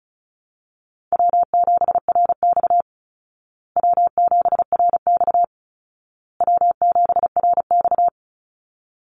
Morse code, heard as a keyed tone, about 34 wpm, 700 Hz. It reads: W7RX W7RX W7RX
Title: Easy Call Signs Repeated 3x - 35wpm